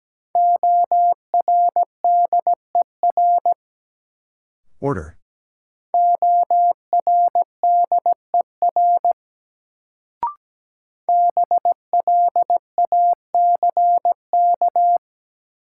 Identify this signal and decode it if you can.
Morse code, heard as a keyed tone, about 17 wpm, 700 Hz. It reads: ORDER ORDER BLACK
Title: Sentences from Top 400 Words - New Word Drill - 17wpm